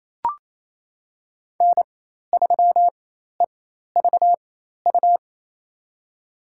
Morse code, heard as a keyed tone, about 28 wpm, spaced as for 13 wpm, 700 Hz. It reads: N3EVU
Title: USA Call Signs - 5 Characters - 4x Character Spacing - 28wpm